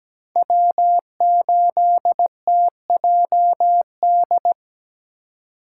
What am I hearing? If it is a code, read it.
Morse code, heard as a keyed tone, about 20 wpm, 700 Hz. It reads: W8TJD
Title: Speed Racing - USA 1x3 Call Signs 17wpm